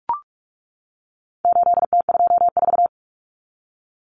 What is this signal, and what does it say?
Morse code, heard as a keyed tone, about 44 wpm, 700 Hz. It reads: 8T24
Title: Common Cut Numbers - 4-Digit 45wpm